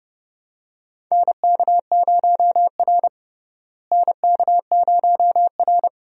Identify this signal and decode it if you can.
Morse code, heard as a keyed tone, about 30 wpm, 700 Hz. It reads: NK0R NK0R
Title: USA Call Signs - 4 Characters - Repeated 2x - 30wpm